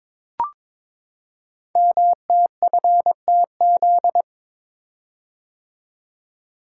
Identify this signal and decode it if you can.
Morse code, heard as a keyed tone, about 22 wpm, 700 Hz. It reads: MTFTZ
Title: Code Groups - 5 Letter:Numbers - 22wpm